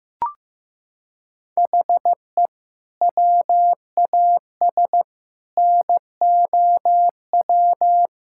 Morse code, heard as a keyed tone, about 15 wpm, 700 Hz. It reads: HE WAS NOW
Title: Sentences from Top 1600 Words 15wpm